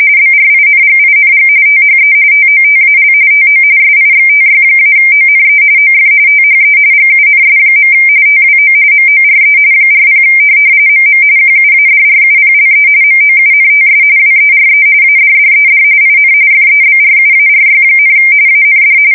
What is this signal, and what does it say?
RTTY, about 45.45 baud. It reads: BRINGING THE TOTAL TO SIXTY-ONE OF THE GUNS. TWO MORE 40 MM QUADRUPLE MOUNTS WERE ADDED IN FEBRUARY, AND
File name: fsk170 / USS-Massachusetts